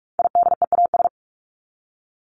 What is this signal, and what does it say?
Morse code, heard as a keyed tone, about 44 wpm, 700 Hz. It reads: IDEAS
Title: ES:S Suffix - Top 100 Words with Base Words 45wpm